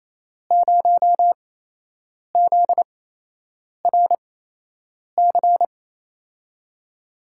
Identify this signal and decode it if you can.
Morse code, heard as a keyed tone, about 28 wpm, spaced as for 8 wpm, 700 Hz. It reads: 0ZRC
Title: USA Call Signs - 6 Characters - 8x Character Spacing - 28wpm